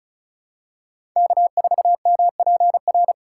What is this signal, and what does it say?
Morse code, heard as a keyed tone, about 36 wpm, 700 Hz. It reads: K4MPR
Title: USA Call Signs - 5 Characters - 4x Character Spacing - 35wpm